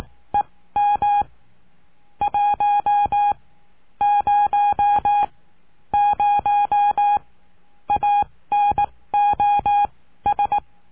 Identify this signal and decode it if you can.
Morse code, heard as a keyed tone, about 18 wpm, spaced as for 12 wpm, 820 Hz. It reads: EM 100 ANOS
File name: cursos / CURSO DE TELEGRAFIA / Telegrafia - Trilha 20